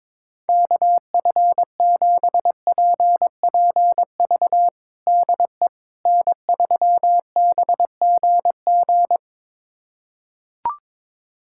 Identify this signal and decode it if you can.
Morse code, heard as a keyed tone, about 22 wpm, 700 Hz. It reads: KF7PPV DE N3BGG
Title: USA Callsign DE Callsign 22wpm